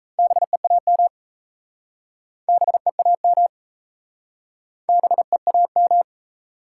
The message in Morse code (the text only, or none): BEAM BEAM BEAM